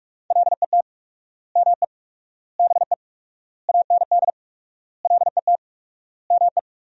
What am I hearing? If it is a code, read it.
Morse code, heard as a keyed tone, about 41 wpm, 700 Hz. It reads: LET ME BE AND LET ME
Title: 4x Word-Spacing - Sentences from Top 300 Words - 45wpm